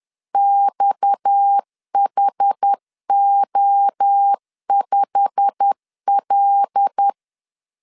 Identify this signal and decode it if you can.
Morse code, heard as a keyed tone, about 10 wpm, 800 Hz. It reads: XHO5L
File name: LEZIONI / 8 YJ45 / 8 YJ45 10wpm 2549